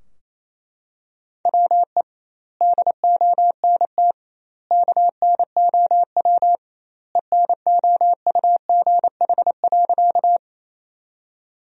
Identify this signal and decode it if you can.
Morse code, heard as a keyed tone, about 28 wpm, 700 Hz. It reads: WE DONT KNOW ENOUGH.
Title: 2x Word-Spacing - Sentences from Top 400 Words - 28wpm